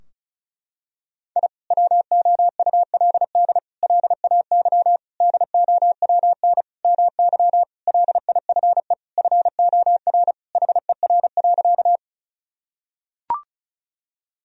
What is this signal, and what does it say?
Morse code, heard as a keyed tone, about 34 wpm, 700 Hz. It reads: I WOULD LAY DOWN MY LIFE FOR HER.
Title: Sentences from Top 1000 Words - Review Part 2 - 35wpm